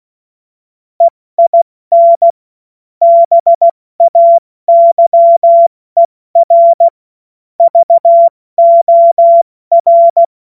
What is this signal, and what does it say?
Morse code, heard as a keyed tone, about 16 wpm, 680 Hz. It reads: EIN BAYER VOR
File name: Witz-12-August-2025-70-F80